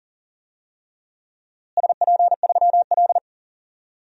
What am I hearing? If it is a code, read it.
Morse code, heard as a keyed tone, about 37 wpm, 700 Hz. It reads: SP3L
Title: DXCC-100 Call Signs - 4 Characters - Repeated 2x - 40wpm